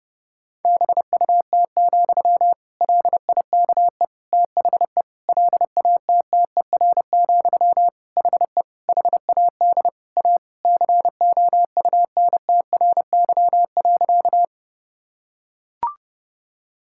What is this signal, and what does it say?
Morse code, heard as a keyed tone, about 30 wpm, 700 Hz. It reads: BUT, LIKE THE LATTER, HE HAD A COUNTRY.
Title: Sentences from Top 1600 Words 30wpm